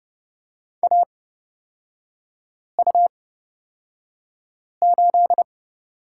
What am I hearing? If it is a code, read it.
Morse code, heard as a keyed tone, about 29 wpm, spaced as for 5 wpm, 700 Hz. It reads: AU8
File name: Lesson 38e - Letter q - Three Characters 30wpm